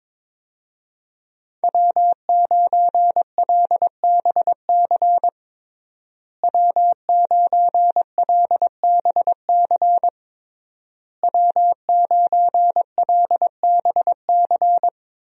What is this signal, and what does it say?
Morse code, heard as a keyed tone, about 22 wpm, 700 Hz. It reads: W9LBC W9LBC W9LBC
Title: USA Call Signs - 5 Characters - Repeated 3x - 22wpm